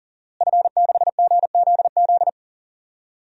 Morse code, heard as a keyed tone, about 40 wpm, 700 Hz. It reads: F6GZZ
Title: DXCC-100 Call Signs - 5 Characters - Repeated 3x - 40wpm